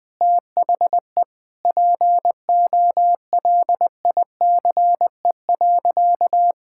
Morse code, heard as a keyed tone, about 21 wpm, 700 Hz. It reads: THE POLICE.
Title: Sentences with 6-Word Phrases - 20wpm